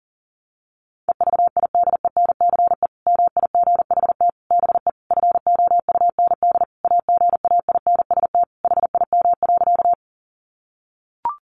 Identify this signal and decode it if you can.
Morse code, heard as a keyed tone, about 40 wpm, 700 Hz. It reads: EVIDENCE MIGHT BE FOUND AGAINST HIM.
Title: Sentences from Top 1300 Words 40wpm